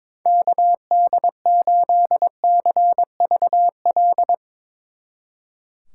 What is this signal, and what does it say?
Morse code, heard as a keyed tone, about 24 wpm, 700 Hz. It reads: KD8CVL
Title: Speed Racing - USA 2x3 Call Signs 22wpm